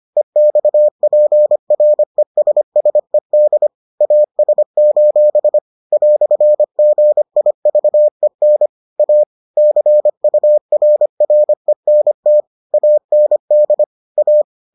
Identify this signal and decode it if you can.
Morse code, heard as a keyed tone, about 25 wpm, 600 Hz. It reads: EXPRESSED AS: "GIVEN A CURRENT AND A